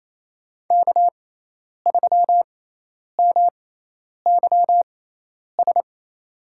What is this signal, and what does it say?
Morse code, heard as a keyed tone, about 27 wpm, spaced as for 10 wpm, 700 Hz. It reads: K3MYS
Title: USA Call Signs - 5 Characters - 6x Character Spacing - 28wpm